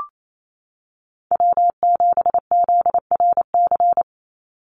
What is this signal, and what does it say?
Morse code, heard as a keyed tone, about 28 wpm, 700 Hz. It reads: W7ZRC
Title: Call Signs Repeated 3x - 28wpm